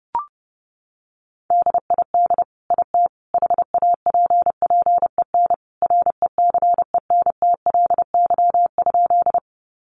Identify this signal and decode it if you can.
Morse code, heard as a keyed tone, about 28 wpm, 700 Hz. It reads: DID IT HAPPEN RECENTLY?
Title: Sentences from Top 1200 Words 30wpm